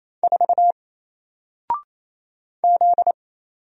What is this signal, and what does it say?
Morse code, heard as a keyed tone, about 28 wpm, 700 Hz. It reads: 4 Z
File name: Single-Letter-Number-28wpm